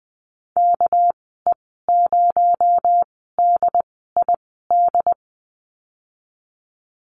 Morse code, heard as a keyed tone, about 20 wpm, spaced as for 15 wpm, 700 Hz. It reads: KE0DID
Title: 238 Lesson 38g - Letter q - Call Signs 20wpm